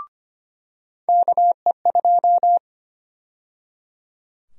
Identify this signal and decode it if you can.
Morse code, heard as a keyed tone, about 25 wpm, 700 Hz. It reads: KE2
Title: USA Call Sign Prefixes - 25wpm